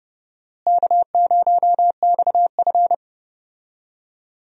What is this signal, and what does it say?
Morse code, heard as a keyed tone, about 30 wpm, 700 Hz. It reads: K0XF
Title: USA Call Signs - 4 Characters - Repeated 2x - 30wpm